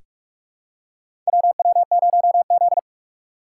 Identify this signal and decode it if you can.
Morse code, heard as a keyed tone, about 43 wpm, 700 Hz. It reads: WW0Z
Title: USA Call Signs - 4x Character Spacing - 45wpm